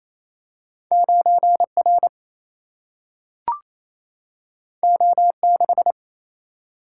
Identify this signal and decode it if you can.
Morse code, heard as a keyed tone, about 28 wpm, 700 Hz. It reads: 9R O6
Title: Single-Letter Single-Number 28wpm